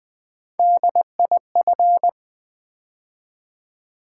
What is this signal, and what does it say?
Morse code, heard as a keyed tone, about 20 wpm, 700 Hz. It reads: DIF
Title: Build A Word - 3 Letters At A Time - 9-Letter Words - 20wpm